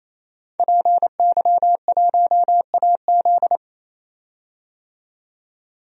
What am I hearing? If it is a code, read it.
Morse code, heard as a keyed tone, about 27 wpm, 700 Hz. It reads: PY1AZ
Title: DXCC-100 Call Signs - 5 Characters - 28wpm